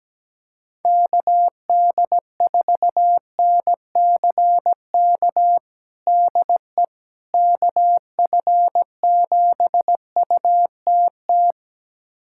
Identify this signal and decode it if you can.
Morse code, heard as a keyed tone, about 17 wpm, 700 Hz. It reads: KD4NCK DE KF7UTT
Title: USA Callsign DE Callsign 17wpm